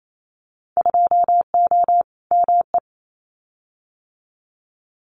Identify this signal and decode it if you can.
Morse code, heard as a keyed tone, about 28 wpm, 700 Hz. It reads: 2O ME
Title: Winter Field Day Exchanges 28wpm